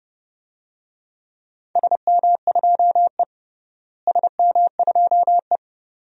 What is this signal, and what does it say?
Morse code, heard as a keyed tone, about 32 wpm, 700 Hz. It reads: SM2E SM2E
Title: DXCC-100 Call Signs - 4 Characters - Repeated 2x - 30wpm